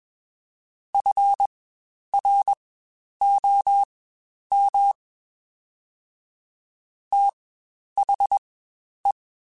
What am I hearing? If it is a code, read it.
Morse code, heard as a keyed tone, about 21 wpm, spaced as for 10 wpm, 800 Hz. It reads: FROM THE
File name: Spaced Characters / Spaced Characters-mp3s / Weather 4 24 sp @20